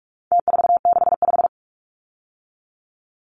Morse code, heard as a keyed tone, about 44 wpm, 700 Hz. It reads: T465
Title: Numbers - Contest Style - 4-Digit 45wpm